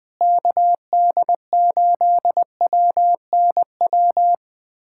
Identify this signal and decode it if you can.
Morse code, heard as a keyed tone, about 20 wpm, 700 Hz. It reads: KD8WNW
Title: USA Callsign DE Callsign - 6 Characters - 20wpm